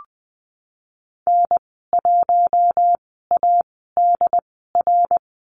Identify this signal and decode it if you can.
Morse code, heard as a keyed tone, about 20 wpm, spaced as for 15 wpm, 700 Hz. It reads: N1ADR
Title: 069 Lesson 12g - Letter d - Call Signs 20wpm